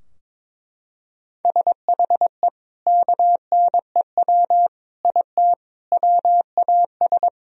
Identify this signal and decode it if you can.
Morse code, heard as a keyed tone, about 22 wpm, 700 Hz. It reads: SHE KNEW IT WAS
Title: Sentences from Top 1000 Words - Review Part 2 - 22wpm